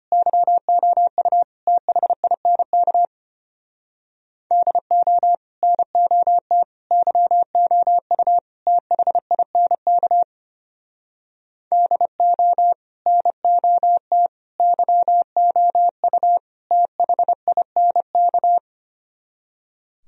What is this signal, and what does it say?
Morse code, heard as a keyed tone, about 33 wpm, 700 Hz. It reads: YOU THINK DO NOT YOU THINK DO NOT YOU THINK